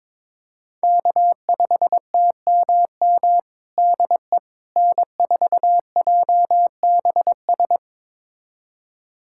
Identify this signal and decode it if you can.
Morse code, heard as a keyed tone, about 22 wpm, 700 Hz. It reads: K5TMM DE N4JBS
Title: USA Callsign DE Callsign - 5 Characters - 22wpm